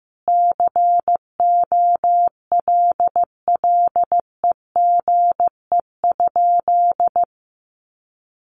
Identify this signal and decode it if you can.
Morse code, heard as a keyed tone, about 15 wpm, 700 Hz. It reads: COLLEGE?
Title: Sentences from Top 1300 Words 15wpm